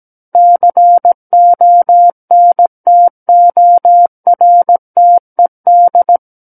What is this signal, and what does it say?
Morse code, heard as a keyed tone, about 17 wpm, 700 Hz. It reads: CONTORTED
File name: Book One / 08-Chapter 8 - 17 WPM